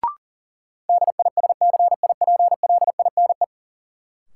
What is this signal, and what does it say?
Morse code, heard as a keyed tone, about 40 wpm, 700 Hz. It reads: DISCIPLINE
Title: Sentences from Top 900 Words - New Word Drill - 40wpm